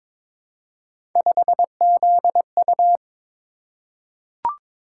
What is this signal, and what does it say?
Morse code, heard as a keyed tone, about 23 wpm, 700 Hz. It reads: 5ZU
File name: practice / 22wpm / Three Letter-Number Combinations - 22wpm